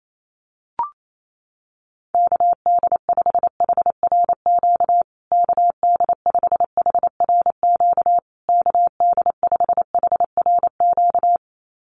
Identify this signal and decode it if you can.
Morse code, heard as a keyed tone, about 28 wpm, 700 Hz. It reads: KD5HRQ KD5HRQ KD5HRQ
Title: USA Call Signs Repeated 3x 28wpm